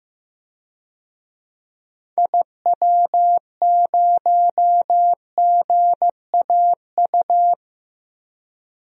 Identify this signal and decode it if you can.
Morse code, heard as a keyed tone, about 15 wpm, 700 Hz. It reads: IW0GAU
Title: DXCC-100 Call Signs - 6 Characters - Repeated 3x - 15wpm